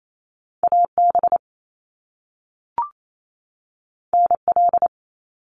Morse code, heard as a keyed tone, about 28 wpm, 700 Hz. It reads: AB NL
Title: Canadian Provinces and Territories 28wpm